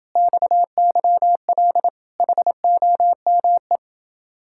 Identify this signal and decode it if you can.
Morse code, heard as a keyed tone, about 27 wpm, 700 Hz. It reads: XYL HOME